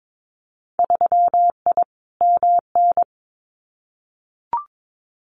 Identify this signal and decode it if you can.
Morse code, heard as a keyed tone, about 22 wpm, 700 Hz. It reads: 3I MN
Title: Winter Field Day Exchanges 22wpm